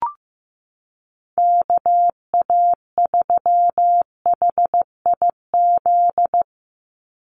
Morse code, heard as a keyed tone, about 15 wpm, 700 Hz. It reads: KA3HIZ
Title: USA Call Signs 15wpm